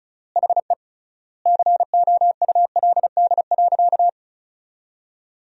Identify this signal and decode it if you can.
Morse code, heard as a keyed tone, about 36 wpm, 700 Hz. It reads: HE COULD.
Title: 3x Word-Spacing - Sentences from Top 500 Words - 35wpm